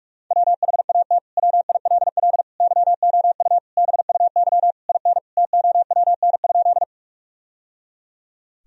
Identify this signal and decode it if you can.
Morse code, heard as a keyed tone, about 41 wpm, 700 Hz. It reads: WHAT WILL YOU BUY IN TOWN?
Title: Sentences from Top 1000 Words - Review Part 4 - 45wpm